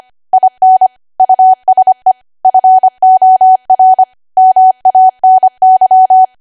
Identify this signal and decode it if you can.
Morse code, heard as a keyed tone, about 25 wpm, 750 Hz. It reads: IN USE FOR MANY
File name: Archive / 25 WPM / 25WPM